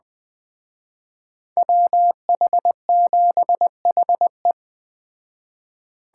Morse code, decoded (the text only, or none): WH7HE